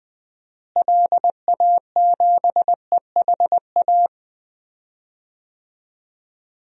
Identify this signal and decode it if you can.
Morse code, heard as a keyed tone, about 21 wpm, 700 Hz. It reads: LA7EHA
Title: DXCC-100 Call Signs - 6 Characters - Repeated 3x - 20wpm